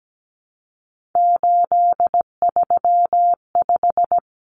Morse code, heard as a keyed tone, about 17 wpm, 700 Hz. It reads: 835
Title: Numbers - 3-Digit 17wpm